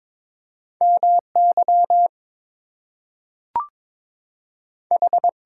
Morse code, heard as a keyed tone, about 22 wpm, 700 Hz. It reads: MY H
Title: Two Letter Words - 22wpm